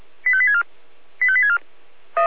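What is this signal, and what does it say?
Five-tone selective call (ZVEI), heard as quick runs of five tones, tones of about 70 ms. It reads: 85864 85864
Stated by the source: ZVEI five-tone call 85356 received on SDR.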